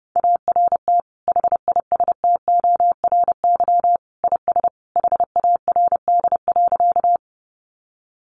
Morse code, heard as a keyed tone, about 28 wpm, 700 Hz. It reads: ART HISTORY IS HARD.
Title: Sentences from Top 600 Words 30wpm